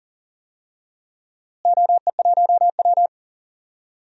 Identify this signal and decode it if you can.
Morse code, heard as a keyed tone, about 38 wpm, 700 Hz. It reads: OE1W
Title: DXCC-100 Call Signs - 4 Characters - Repeated 2x - 40wpm